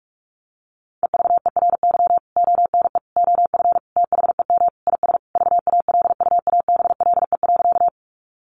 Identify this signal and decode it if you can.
Morse code, heard as a keyed tone, about 44 wpm, 700 Hz. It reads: EVERY ONE OF THEM IS VALUABLE.
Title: Sentences from Top 1100 Words 45wpm